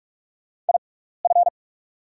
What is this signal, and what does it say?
Morse code, heard as a keyed tone, about 44 wpm, spaced as for 15 wpm, 690 Hz. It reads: IF
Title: 6x Character Spacing - Top 100 Words - 45wpm